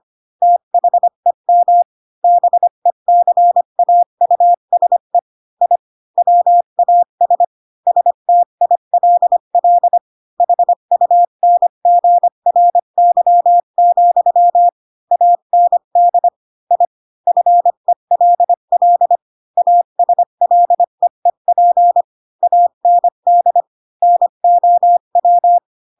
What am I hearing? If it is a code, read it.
Morse code, heard as a keyed tone, about 25 wpm, 700 Hz. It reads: THEM BECAUSE I WAS STILL HUNGRY, AND I FELL ASLEEP AND NOW
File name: Pinocchio / Pinocchio ch7 20 25 ik1hge